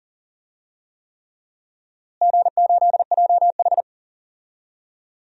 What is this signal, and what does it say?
Morse code, heard as a keyed tone, about 40 wpm, 700 Hz. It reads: G8JH
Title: DXCC-100 Call Signs - 4 Characters - Repeated 3x - 40wpm